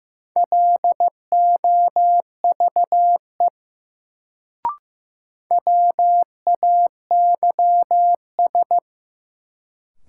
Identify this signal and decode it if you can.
Morse code, heard as a keyed tone, about 15 wpm, 700 Hz. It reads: LOVE WAYS
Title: Sets of 5 Words Encore - New Word Drill - 15wpm